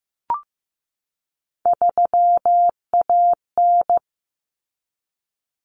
Morse code, heard as a keyed tone, about 15 wpm, 700 Hz. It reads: 3AN
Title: Common Cut Numbers - 3-Digit 15wpm